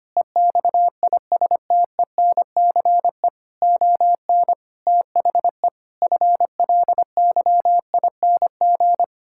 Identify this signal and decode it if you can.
Morse code, heard as a keyed tone, about 25 wpm, 700 Hz. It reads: EXISTENCE ON THE FLYING
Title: Pangram Challenge v2 - 25wpm